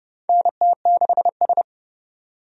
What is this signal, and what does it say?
Morse code, heard as a keyed tone, about 32 wpm, 700 Hz. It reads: NT6S
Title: USA Call Signs - 4 Characters - 30wpm